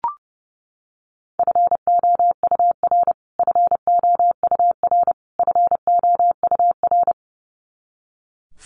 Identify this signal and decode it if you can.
Morse code, heard as a keyed tone, about 30 wpm, 700 Hz. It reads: FOUR FOUR FOUR
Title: Top 300 Words Repeated 30wpm